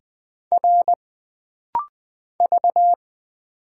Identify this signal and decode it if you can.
Morse code, heard as a keyed tone, about 20 wpm, 700 Hz. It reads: R V
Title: Single Letter-Number Alt - Rapid-Fire 20wpm